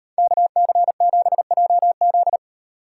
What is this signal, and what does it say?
Morse code, heard as a keyed tone, about 38 wpm, 700 Hz. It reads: KC7JZ